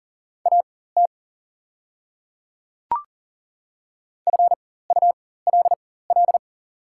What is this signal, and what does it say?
Morse code, heard as a keyed tone, about 40 wpm, spaced as for 19 wpm, 690 Hz. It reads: AT FULL
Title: 4x Character Spacing - Top 500 Words - 40wpm